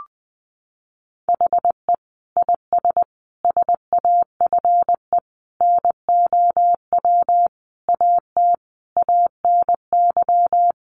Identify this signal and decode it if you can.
Morse code, heard as a keyed tone, about 20 wpm, 700 Hz. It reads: HE IS SAFE NOW AT ANY
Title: Sentences from Top 900 Words 20wpm